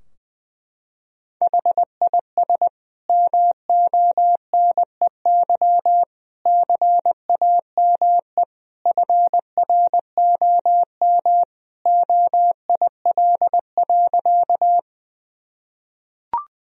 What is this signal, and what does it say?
Morse code, heard as a keyed tone, about 21 wpm, 700 Hz. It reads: HIS MONEY CAME FROM OIL.
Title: Sentences from Top 1000 Words - Review Part 4 - 20wpm